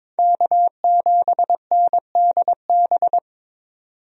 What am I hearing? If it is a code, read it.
Morse code, heard as a keyed tone, about 22 wpm, 700 Hz. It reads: K7NDB
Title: USA Call Signs - 5 Characters - 22wpm